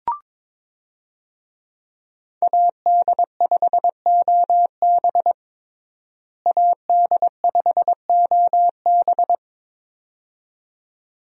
Morse code, heard as a keyed tone, about 22 wpm, 700 Hz. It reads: AD5OB AD5OB
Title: USA Call Signs - 5 Characters - Repeated 2x - 22wpm